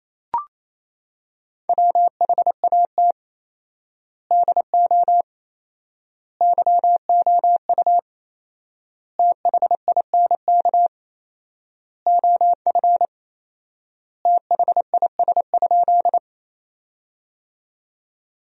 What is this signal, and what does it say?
Morse code, heard as a keyed tone, about 27 wpm, 700 Hz. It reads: WHAT DO YOU THINK OF THIS?
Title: 4x Word-Spacing - Sentences from Top 200 Words - 28wpm